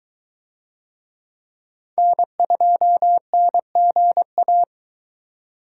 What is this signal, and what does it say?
Morse code, heard as a keyed tone, about 23 wpm, 700 Hz. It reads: N2NGA